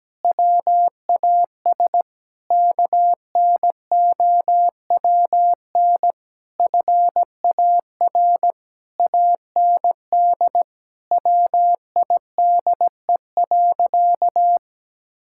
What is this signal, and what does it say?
Morse code, heard as a keyed tone, about 17 wpm, 700 Hz. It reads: WAS KNOWN FAR AND WIDE.
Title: Sentences from Top 1000 Words - Review Part 6 - 17wpm